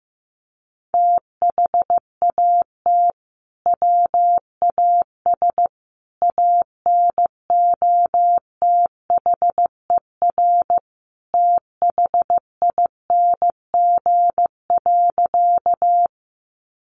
Morse code, heard as a keyed tone, about 14 wpm, 700 Hz. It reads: THAT WAS ANOTHER THING.
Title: Sentences from Top 600 Words 15wpm